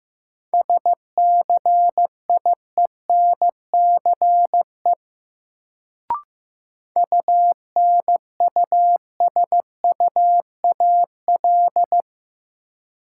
Morse code, heard as a keyed tone, about 15 wpm, 700 Hz. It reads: SCIENCE UNUSUAL
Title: Sentences from Top 1400 Words - New Word Drill Alt - 15wpm